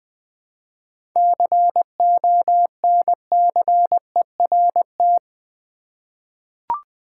Morse code, heard as a keyed tone, about 20 wpm, 700 Hz. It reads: CONCERT
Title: 7 Letter Words - First 3 Letters Given - 20wpm